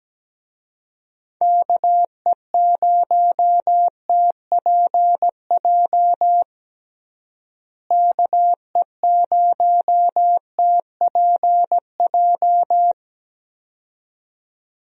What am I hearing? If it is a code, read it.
Morse code, heard as a keyed tone, about 17 wpm, 700 Hz. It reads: KE0TPJ KE0TPJ
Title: USA Call Signs - 6 Characters - Repeated 2x - 17wpm